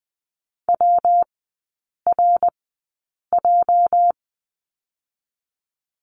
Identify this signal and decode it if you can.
Morse code, heard as a keyed tone, about 20 wpm, spaced as for 8 wpm, 700 Hz. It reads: WRJ
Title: 205 Lesson 33e - Letter j - Three Characters 20wpm